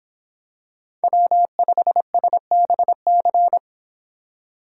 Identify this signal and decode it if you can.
Morse code, heard as a keyed tone, about 25 wpm, 700 Hz. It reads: W5SBC